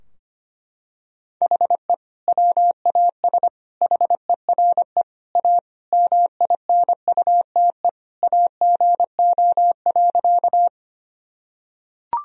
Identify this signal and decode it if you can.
Morse code, heard as a keyed tone, about 25 wpm, 700 Hz. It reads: HE WAS HERE A MINUTE AGO.
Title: Sentences from Top 1000 Words - Review Part 3 - 25wpm